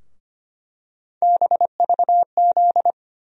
Morse code, heard as a keyed tone, about 25 wpm, 700 Hz. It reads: BVZ
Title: USA Call Sign Postfixes - 3 Letters 25wpm